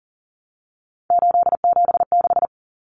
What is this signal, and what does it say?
Morse code, heard as a keyed tone, about 40 wpm, 700 Hz. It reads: 876
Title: Numbers - 3-Digit 40wpm